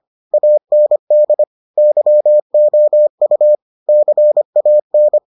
Morse code, heard as a keyed tone, about 25 wpm, 600 Hz. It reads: AND YOU CAN